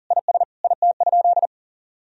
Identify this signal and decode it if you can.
Morse code, heard as a keyed tone, about 40 wpm, 700 Hz. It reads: IS IT?
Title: Sentences from Top 1000 Words - Review Part 4 - 40wpm